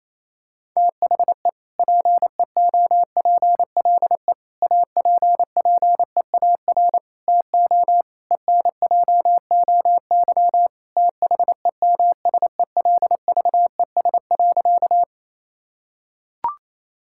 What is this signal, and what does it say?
Morse code, heard as a keyed tone, about 28 wpm, 700 Hz. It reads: THE PEOPLE APPEAR TO ENJOY THEMSELVES.
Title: Sentences from Top 1000 Words - Review Part 5 - 28wpm